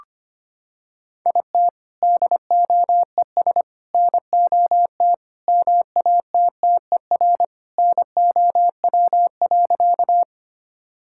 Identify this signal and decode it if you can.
Morse code, heard as a keyed tone, about 25 wpm, 700 Hz. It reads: IT DOES NOT MATTER NOW.
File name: practice / 25wpm / Sentences with 4-Word Phrases - 25wpm